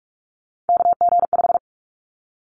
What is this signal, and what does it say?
Morse code, heard as a keyed tone, about 46 wpm, 700 Hz. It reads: KG5
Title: Callsign Prefixes for Top 25 DXCC - 45wpm